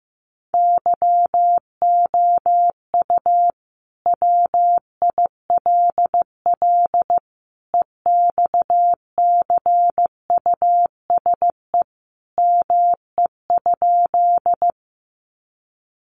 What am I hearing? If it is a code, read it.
Morse code, heard as a keyed tone, about 15 wpm, 700 Hz. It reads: YOU WILL EXCUSE ME?
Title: Sentences from Top 700 Words 15wpm